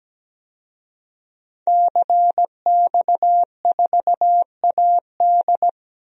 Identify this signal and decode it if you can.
Morse code, heard as a keyed tone, about 17 wpm, 700 Hz. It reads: CX4AD
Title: DXCC-100 Call Signs - 5 Characters - 17wpm